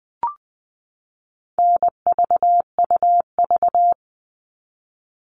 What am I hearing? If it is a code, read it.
Morse code, heard as a keyed tone, about 20 wpm, 700 Hz. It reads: NVUV
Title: All Cut Numbers - 4-Digit 20wpm